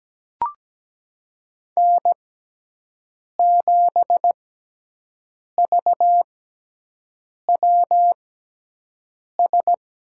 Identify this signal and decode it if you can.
Morse code, heard as a keyed tone, about 17 wpm, spaced as for 6 wpm, 700 Hz. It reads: N7VWS
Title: USA Call Signs - 5 Characters - 6x Character Spacing - 17wpm